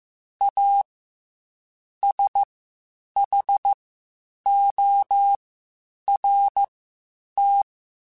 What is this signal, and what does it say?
Morse code, heard as a keyed tone, about 15 wpm, spaced as for 8 wpm, 800 Hz. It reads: A SHORT
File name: Quotes Spaced / Quotes Spaced-mp3s / Temper sp @13